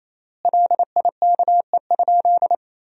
Morse code, heard as a keyed tone, about 28 wpm, 700 Hz. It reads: LIKE?
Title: 3x Word-Spacing - Sentences from Top 100 Words - 28wpm